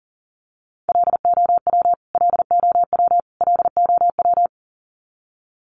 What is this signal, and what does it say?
Morse code, heard as a keyed tone, about 40 wpm, 700 Hz. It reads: LOW LOW LOW
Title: Top 500 Words Repeated 40wpm